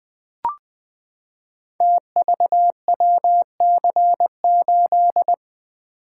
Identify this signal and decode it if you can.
Morse code, heard as a keyed tone, about 20 wpm, 700 Hz. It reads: TVWC8
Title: Code Groups - 5 Letter:Numbers - 20wpm